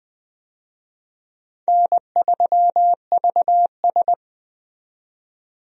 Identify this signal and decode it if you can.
Morse code, heard as a keyed tone, about 20 wpm, 700 Hz. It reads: N3VS
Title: USA Call Signs - 4 Characters - 20wpm